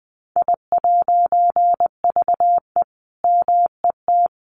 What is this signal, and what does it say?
Morse code, heard as a keyed tone, about 20 wpm, 700 Hz. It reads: I'VE MET
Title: A Sound Of Thunder 20wpm